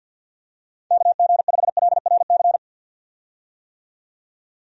Morse code, heard as a keyed tone, about 50 wpm, 700 Hz. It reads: KG5LRC
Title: USA Call Signs - 6 Characters - 50wpm